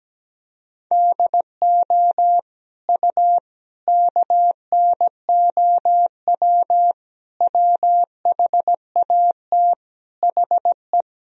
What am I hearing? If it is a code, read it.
Morse code, heard as a keyed tone, about 17 wpm, 700 Hz. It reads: DO U KNOW WHAT HE
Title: Sentences from Top 300 Words and Abbreviations 17wpm